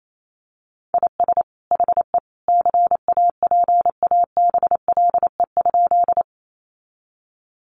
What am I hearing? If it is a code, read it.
Morse code, heard as a keyed tone, about 28 wpm, 700 Hz. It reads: IS HE CAPABLE?
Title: Sentences from Top 1200 Words 28wpm